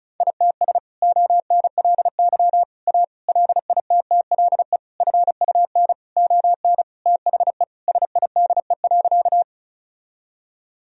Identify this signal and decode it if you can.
Morse code, heard as a keyed tone, about 34 wpm, 700 Hz. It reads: ITS ONLY A LITTLE FUN ON THE SIDE.
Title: Sentences from Top 1000 Words - Review Part 4 - 35wpm